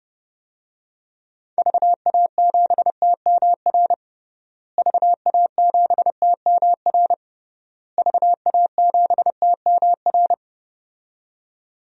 Morse code, heard as a keyed tone, about 31 wpm, 700 Hz. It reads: VA7TMR VA7TMR VA7TMR
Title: DXCC-100 Call Signs - 6 Characters - Repeated 3x - 30wpm